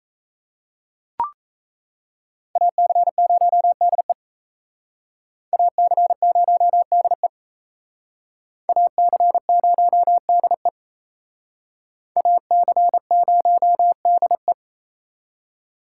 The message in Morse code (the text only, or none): AC0DE AC0DE AC0DE AC0DE